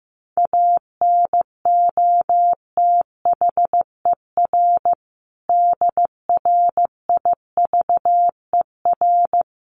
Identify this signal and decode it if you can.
Morse code, heard as a keyed tone, about 15 wpm, 700 Hz. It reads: ANOTHER DRIVER
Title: Sentences from Top 1400 Words 15wpm